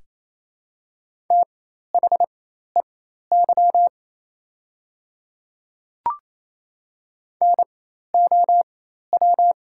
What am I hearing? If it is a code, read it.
Morse code, heard as a keyed tone, about 28 wpm, spaced as for 13 wpm, 700 Hz. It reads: THEY NOW
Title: 4x Character Spacing - Top 100 Words - 28wpm